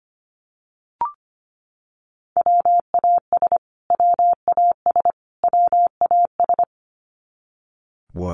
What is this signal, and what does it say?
Morse code, heard as a keyed tone, about 25 wpm, 700 Hz. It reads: WAS WAS WAS
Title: Top 200 Words Repeated 25wpm